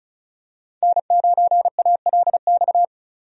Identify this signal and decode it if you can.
Morse code, heard as a keyed tone, about 36 wpm, 700 Hz. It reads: N9ALX
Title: USA Call Signs - 5 Characters - 35wpm